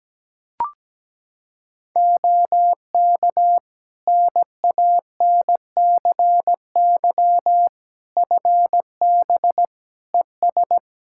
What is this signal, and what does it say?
Morse code, heard as a keyed tone, about 17 wpm, 700 Hz. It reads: OK NANCY FB ES
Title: Practice QSO - Exchange 2 - Info - 17wpm